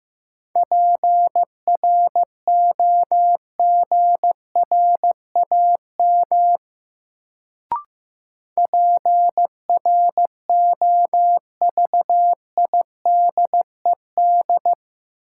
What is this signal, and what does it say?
Morse code, heard as a keyed tone, about 15 wpm, 700 Hz. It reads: PROGRAM PROVIDED
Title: Sentences from Top 1500 Words - New Word Drill Alt - 15wpm